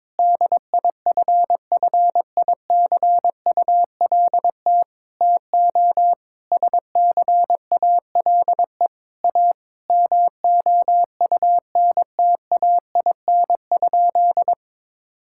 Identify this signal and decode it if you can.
Morse code, heard as a keyed tone, about 22 wpm, 700 Hz. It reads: DIFFICULT TO SCALE A MOUNTAIN?
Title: Sentences from Top 1000 Words - Review Part 5 - 22wpm